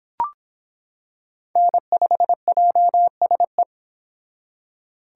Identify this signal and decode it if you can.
Morse code, heard as a keyed tone, about 25 wpm, 700 Hz. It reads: N5JSE